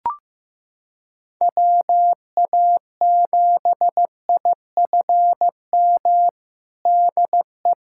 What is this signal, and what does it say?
Morse code, heard as a keyed tone, about 15 wpm, 700 Hz. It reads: WA7IFM DE
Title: USA Callsign DE Callsign - 6 Characters - 15wpm